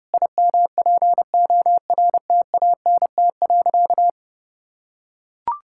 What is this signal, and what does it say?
Morse code, heard as a keyed tone, about 30 wpm, 700 Hz. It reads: IMPORTANT.
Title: Sentences from Top 1000 Words - Review Part 6 - 30wpm